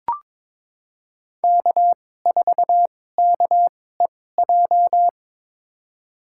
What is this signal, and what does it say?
Morse code, heard as a keyed tone, about 22 wpm, spaced as for 16 wpm, 700 Hz. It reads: K4KEJ
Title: USA Call Signs - 5 Characters - 2x Character Spacing - 22wpm